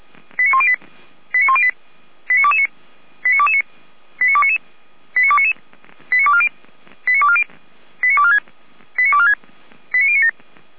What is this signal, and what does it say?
Five-tone selective call (ZVEI), heard as quick runs of five tones, tones of about 70 ms. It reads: 89108 89108 89229 89229 89299 89299 89240 89240 89246 89246 89097